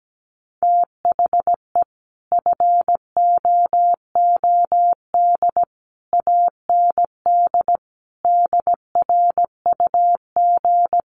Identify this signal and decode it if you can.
Morse code, heard as a keyed tone, about 17 wpm, 700 Hz. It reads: THE FOOD AND DRUG
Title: Sets of 5 Words 17wpm